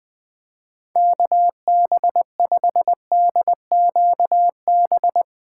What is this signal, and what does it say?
Morse code, heard as a keyed tone, about 21 wpm, 700 Hz. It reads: KB5DQB
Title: Speed Racing - USA Call Signs 17wpm